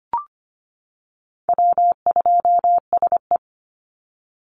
Morse code, heard as a keyed tone, about 25 wpm, 700 Hz. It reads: W2SE
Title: NA SOTA Chaser Callsigns 25wpm